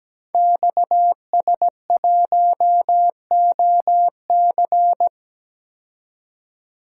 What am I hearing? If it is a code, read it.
Morse code, heard as a keyed tone, about 17 wpm, 700 Hz. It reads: XS1OC
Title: Code Groups - 5 Letter:Numbers - 17wpm